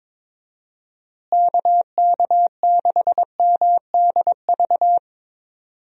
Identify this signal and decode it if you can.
Morse code, heard as a keyed tone, about 22 wpm, 700 Hz. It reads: KK6MDV
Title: USA Call Signs - 6 Characters - Repeated 3x - 22wpm